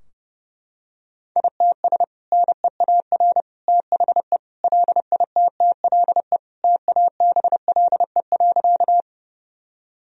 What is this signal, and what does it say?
Morse code, heard as a keyed tone, about 29 wpm, 700 Hz. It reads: ITS NEAR THE LITTLE TABLE.
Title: Sentences from Top 1000 Words - Review Part 2 - 30wpm